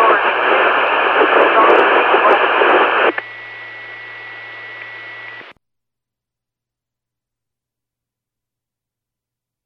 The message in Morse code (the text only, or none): none